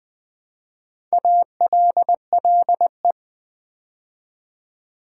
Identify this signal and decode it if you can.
Morse code, heard as a keyed tone, about 20 wpm, 700 Hz. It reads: ALLE
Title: Build A Word - 2 Letters At A Time - 6-Letter Words - 20wpm